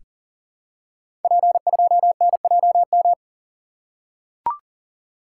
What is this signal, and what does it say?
Morse code, heard as a keyed tone, about 40 wpm, 700 Hz. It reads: P2NJM
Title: Code Groups - 5 Letter:Numbers - 40wpm